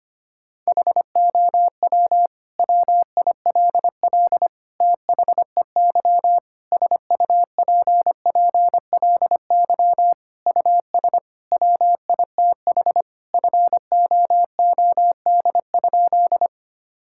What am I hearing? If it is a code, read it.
Morse code, heard as a keyed tone, about 25 wpm, 700 Hz. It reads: HOW WILL THEY SUPPLY US WITH FOOD?
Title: Sentences from Top 1000 Words - Review Part 5 - 25wpm